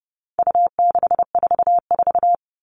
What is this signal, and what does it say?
Morse code, heard as a keyed tone, about 30 wpm, 700 Hz. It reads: U644
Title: All Cut Numbers - 4-Digit 30wpm